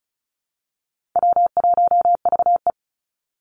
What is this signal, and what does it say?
Morse code, heard as a keyed tone, about 36 wpm, 700 Hz. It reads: W1VE
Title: Call Signs - 35wpm